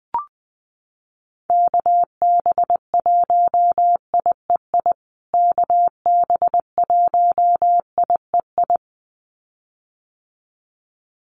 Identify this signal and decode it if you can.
Morse code, heard as a keyed tone, about 20 wpm, 700 Hz. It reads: KB1IEI KB1IEI
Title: USA Call Signs Repeated 2x 20wpm